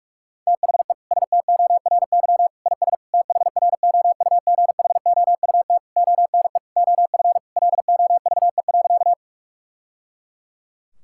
Morse code, heard as a keyed tone, about 46 wpm, 700 Hz. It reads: THE STORY IS THROUGHOUT ONE OF LOVE.
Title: Sentences from Top 1000 Words - Review Part 5 - 45wpm